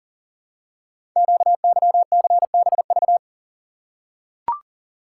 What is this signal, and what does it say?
Morse code, heard as a keyed tone, about 40 wpm, 690 Hz. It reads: QYCBV
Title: Code Groups - 5 Letters - 40wpm